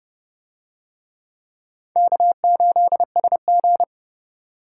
Morse code, heard as a keyed tone, about 26 wpm, 700 Hz. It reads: K8SG